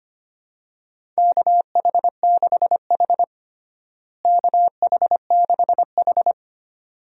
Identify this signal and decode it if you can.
Morse code, heard as a keyed tone, about 25 wpm, 700 Hz. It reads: KH6H KH6H
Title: DXCC-100 Call Signs - 4 Characters - Repeated 2x - 25wpm